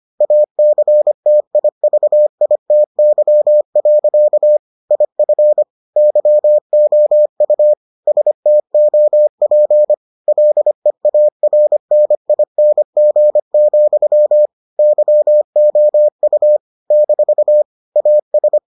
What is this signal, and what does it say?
Morse code, heard as a keyed tone, about 25 wpm, 600 Hz. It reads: ACTIVITY. IF YOU STOP LEARNING, YOU - AS